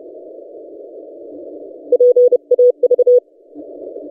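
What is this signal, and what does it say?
Morse code, heard as a keyed tone, about 30 wpm, 490 Hz. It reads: PAV
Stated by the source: DX station 3D2GC on 20 metres CW, recorded by N6ML.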